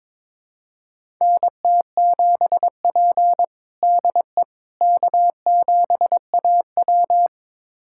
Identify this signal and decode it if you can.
Morse code, heard as a keyed tone, about 22 wpm, 700 Hz. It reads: NT7P DE K7AW
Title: USA Callsign DE Callsign - 4 Characters - 22wpm